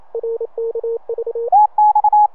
Morse code, heard as a keyed tone, about 28 wpm, 460 Hz. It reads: RKV